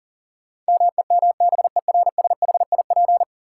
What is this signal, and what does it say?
Morse code, heard as a keyed tone, about 40 wpm, 700 Hz. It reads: MEMBERSHIP
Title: Sentences from Top 1400 Words - New Word Drill - 40wpm